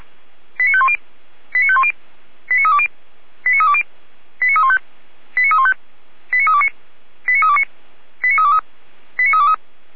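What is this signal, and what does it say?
Five-tone selective call (ZVEI), heard as quick runs of five tones, tones of about 70 ms. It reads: 89510 89510 89320 89320 89315 89315 89329 89329 89323 89323
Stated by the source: Five-tone ZVEI selective call 89509, received by SDR, fire brigade Orlamünde (feuerwehr-orlamuende).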